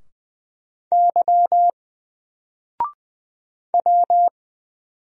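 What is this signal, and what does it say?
Morse code, heard as a keyed tone, about 20 wpm, 700 Hz. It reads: Y W
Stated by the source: Single letter and number Morse drill at 20 wpm, from N4LCD.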